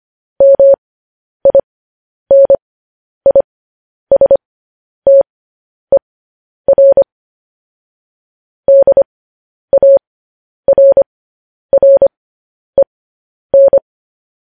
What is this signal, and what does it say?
Morse code, heard as a keyed tone, about 25 wpm, spaced as for 10 wpm, 550 Hz. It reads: MINISTER DARREN
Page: news headlines in morse code at 10 wpm